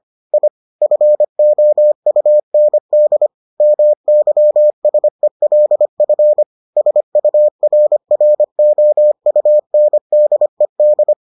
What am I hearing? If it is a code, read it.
Morse code, heard as a keyed tone, about 25 wpm, 600 Hz. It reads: I FOUND MYSELF SURROUNDED